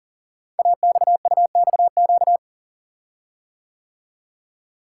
Morse code, heard as a keyed tone, about 38 wpm, 690 Hz. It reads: AXUXQ
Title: Code Groups - 5 Letters - 40wpm